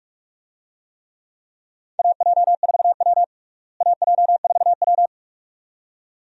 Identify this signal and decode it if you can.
Morse code, heard as a keyed tone, about 43 wpm, 700 Hz. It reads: AJ4W AJ4W
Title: USA Call Signs - 4 Characters - Repeated 2x - 45wpm